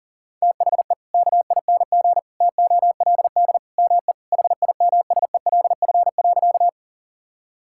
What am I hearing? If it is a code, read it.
Morse code, heard as a keyed tone, about 40 wpm, 700 Hz. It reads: THE KING TOLD ME HIMSELF.
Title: Sentences from Top 1600 Words 40wpm